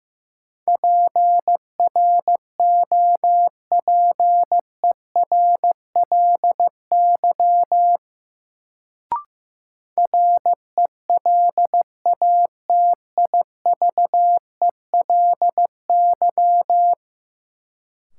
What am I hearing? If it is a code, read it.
Morse code, heard as a keyed tone, about 15 wpm, 700 Hz. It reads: PROPERLY RELATIVELY
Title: Sentences from Top 1100 Words - New Word Drill - 15wpm